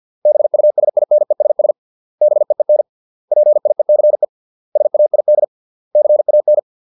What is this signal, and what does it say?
Morse code, heard as a keyed tone, about 50 wpm, 600 Hz. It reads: BUSINESS BEEN PIECE SAID CAN